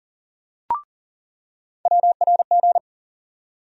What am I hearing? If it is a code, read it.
Morse code, heard as a keyed tone, about 40 wpm, 700 Hz. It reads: WRG
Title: USA Call Sign Postfixes - 3 Letters 40wpm